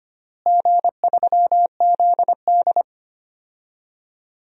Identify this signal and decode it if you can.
Morse code, heard as a keyed tone, about 25 wpm, 700 Hz. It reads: G3ZD
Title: DXCC-100 Call Signs - 4 Characters - Repeated 3x - 25wpm